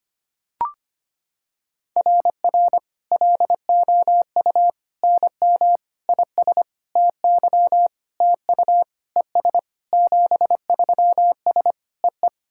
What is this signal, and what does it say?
Morse code, heard as a keyed tone, about 25 wpm, 700 Hz. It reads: RR LOU NM IS TY TU ES 73S E E
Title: SOTA QSO Practice 25wpm